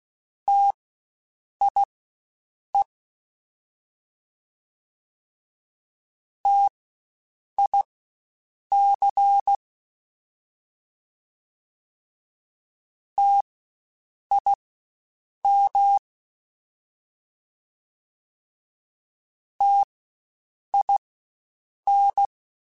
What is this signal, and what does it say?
Morse code, heard as a keyed tone, about 16 wpm, spaced as for 7 wpm, 800 Hz. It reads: TIE TIC TIM TIN
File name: Spaced Alphas / Spaced Alphas mp3s / Ts sp @15